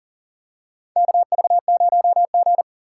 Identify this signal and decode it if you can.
Morse code, heard as a keyed tone, about 40 wpm, 700 Hz. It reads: KV0G
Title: USA Call Signs - 4 Characters - 4x Character Spacing - 40wpm